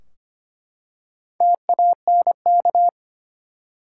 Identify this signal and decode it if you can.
Morse code, heard as a keyed tone, about 25 wpm, 700 Hz. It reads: TANK
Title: Four Letter Words - 25wpm